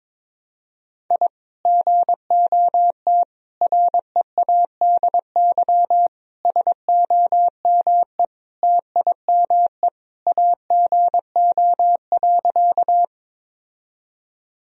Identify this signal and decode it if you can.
Morse code, heard as a keyed tone, about 22 wpm, 700 Hz. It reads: I GOT READY SOME TIME AGO.
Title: Sentences from Top 1000 Words - Review Part 3 - 22wpm